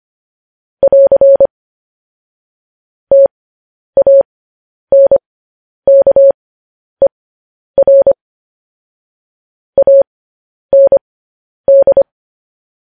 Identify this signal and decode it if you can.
Morse code, heard as a keyed tone, about 25 wpm, spaced as for 10 wpm, 550 Hz. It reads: <AR> TANKER AND
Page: News Headlines in Morse Code at 10 WPM